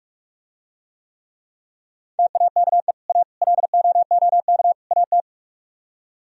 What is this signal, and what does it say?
Morse code, heard as a keyed tone, about 49 wpm, 700 Hz. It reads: TAKE A LOOK AT